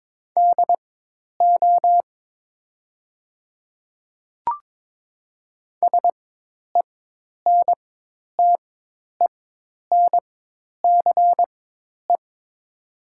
Morse code, heard as a keyed tone, about 22 wpm, spaced as for 10 wpm, 700 Hz. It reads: DO SENTENCE
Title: 4x Character Spacing - Top 200 Words - 22wpm